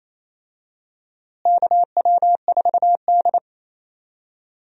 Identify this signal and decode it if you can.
Morse code, heard as a keyed tone, about 28 wpm, 700 Hz. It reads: KW4D
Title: USA Call Signs - 4 Characters - Repeated 3x - 28wpm